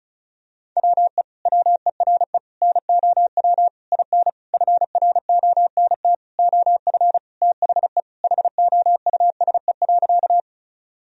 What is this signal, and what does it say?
Morse code, heard as a keyed tone, about 36 wpm, 700 Hz. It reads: WE WERE NOW IN FRONT OF THE HOUSE.
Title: Sentences from Top 1000 Words - Review Part 2 - 35wpm